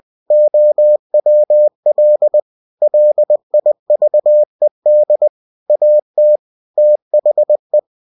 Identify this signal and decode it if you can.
Morse code, heard as a keyed tone, about 20 wpm, 600 Hz. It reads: OWL LIVED AT THE
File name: cw-books / winnie-the-pooh / winnie-20wpm-page-0028